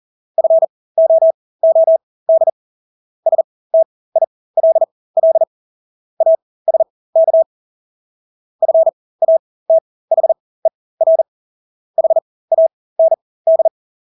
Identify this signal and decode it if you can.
Morse code, heard as a keyed tone, about 40 wpm, spaced as for 20 wpm, 650 Hz. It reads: FOOD STILL ASK FATHER HAND
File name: Engelske ord 40-20 wpm